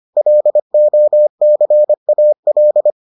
Morse code, heard as a keyed tone, about 25 wpm, 600 Hz. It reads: LOCAL